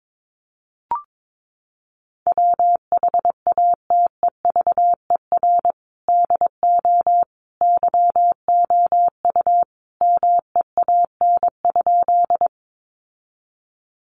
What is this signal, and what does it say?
Morse code, heard as a keyed tone, about 22 wpm, 700 Hz. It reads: WHATEVER DO YOU MEAN?
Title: Sentences from Top 1100 Words 22wpm